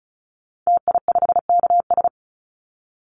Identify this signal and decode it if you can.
Morse code, heard as a keyed tone, about 34 wpm, 700 Hz. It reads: TI5KS
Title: Call Signs Repeated 2x - 35wpm